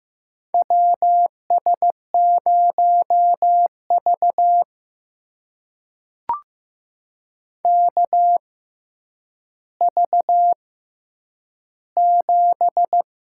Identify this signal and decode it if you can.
Morse code, heard as a keyed tone, about 15 wpm, spaced as for 9 wpm, 700 Hz. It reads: WS0V KV7
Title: USA Call Signs - 4 Characters - 6x Character Spacing - 15wpm